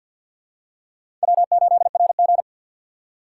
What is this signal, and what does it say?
Morse code, heard as a keyed tone, about 51 wpm, 700 Hz. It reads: W8RG
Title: USA Call Signs - 3x Character Spacing - 50wpm